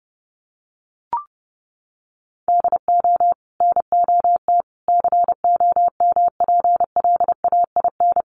Morse code, heard as a keyed tone, about 30 wpm, 700 Hz. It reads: DO NOT COMPLAIN
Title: Sentences from Top 1300 Words 30wpm